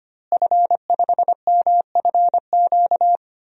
Morse code, heard as a keyed tone, about 25 wpm, 700 Hz. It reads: F5MFQ
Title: DXCC-100 Call Signs - 5 Characters - Repeated 2x - 25wpm